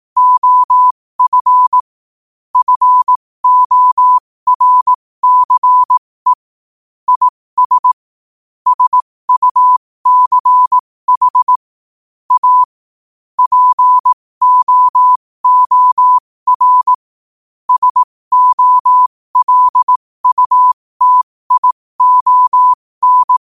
Morse code, heard as a keyed tone, about 18 wpm, spaced as for 15 wpm, 1000 Hz. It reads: OF FORCE IS SUCH A POOR SOLUTION
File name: qotdfiles / 15wpm